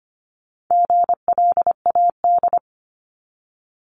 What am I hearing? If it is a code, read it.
Morse code, heard as a keyed tone, about 25 wpm, 700 Hz. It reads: GLAD
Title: Multi-letter Phonemes 25wpm